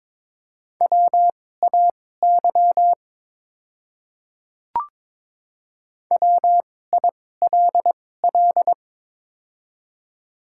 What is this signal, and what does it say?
Morse code, heard as a keyed tone, about 23 wpm, spaced as for 16 wpm, 700 Hz. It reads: WAY WILL
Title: 2x Character Spacing - Top 100 Words - 22wpm